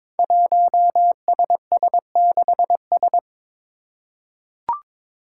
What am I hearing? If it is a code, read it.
Morse code, heard as a keyed tone, about 22 wpm, 700 Hz. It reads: 1SS6S
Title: Five Challenging Letter-Number Combinations 22wpm